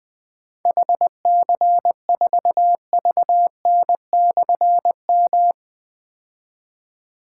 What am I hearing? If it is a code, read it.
Morse code, heard as a keyed tone, about 21 wpm, 700 Hz. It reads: HC4VN/M
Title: Difficult Call Signs 20wpm